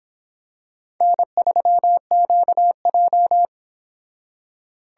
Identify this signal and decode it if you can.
Morse code, heard as a keyed tone, about 25 wpm, 700 Hz. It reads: N3QJ